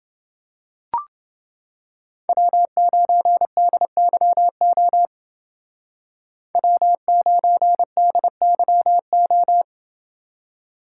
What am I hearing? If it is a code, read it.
Morse code, heard as a keyed tone, about 29 wpm, 700 Hz. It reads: W9DYO W9DYO